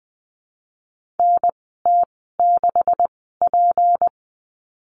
Morse code, heard as a keyed tone, about 20 wpm, spaced as for 15 wpm, 700 Hz. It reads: NT6P
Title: 154 Lesson 25g - Letter p - Call Signs 20wpm